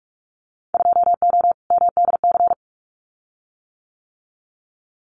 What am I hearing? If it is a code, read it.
Morse code, heard as a keyed tone, about 44 wpm, 700 Hz. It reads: 2O MDC
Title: Winter Field Day Exchanges 45wpm